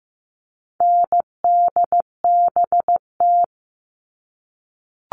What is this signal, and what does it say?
Morse code, heard as a keyed tone, about 15 wpm, 700 Hz. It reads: NDBT
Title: All Cut Numbers - 4-Digit 15wpm